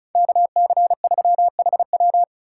Morse code, heard as a keyed tone, about 47 wpm, spaced as for 41 wpm, 700 Hz. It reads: KC3HW
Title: Speed Racing - USA 2x2 Call Signs 35wpm